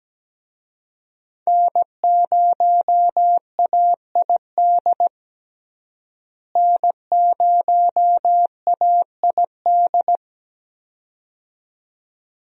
Morse code, heard as a keyed tone, about 17 wpm, 700 Hz. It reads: N0AID N0AID
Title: USA Call Signs - 5 Characters - Repeated 2x - 17wpm